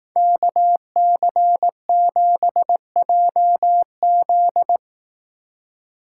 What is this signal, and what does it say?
Morse code, heard as a keyed tone, about 18 wpm, 700 Hz. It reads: KC7JZ
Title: Speed Racing - USA 2x2 Call Signs 15wpm